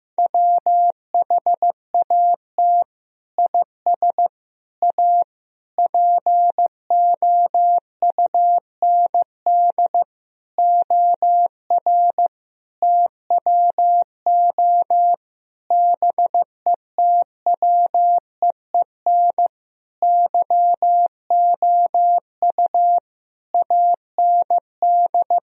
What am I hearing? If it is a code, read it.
Morse code, heard as a keyed tone, about 15 wpm, 700 Hz. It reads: WHAT IS A POUND OR TWO BETWEEN YOU AND
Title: Sentences from Top 1000 Words - Review Part 2 - 15wpm